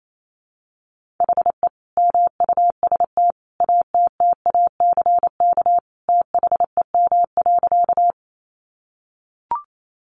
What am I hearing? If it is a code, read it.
Morse code, heard as a keyed tone, about 28 wpm, 700 Hz. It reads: HE MUST ATTACK THEM.
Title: Sentences from Top 1500 Words 28wpm